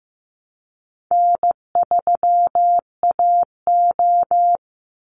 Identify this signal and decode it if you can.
Morse code, heard as a keyed tone, about 15 wpm, 700 Hz. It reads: N3AO
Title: Easy Call Signs - 15wpm